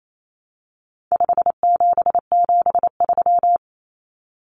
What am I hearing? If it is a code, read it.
Morse code, heard as a keyed tone, about 28 wpm, 700 Hz. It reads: 5773
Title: Common Cut Numbers - 4-Digit 28wpm